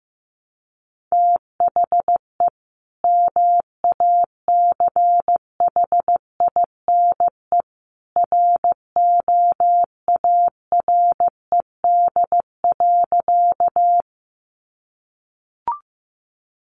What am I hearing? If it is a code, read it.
Morse code, heard as a keyed tone, about 15 wpm, 700 Hz. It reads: THE MACHINE ROARED.
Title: A Sound Of Thunder 15wpm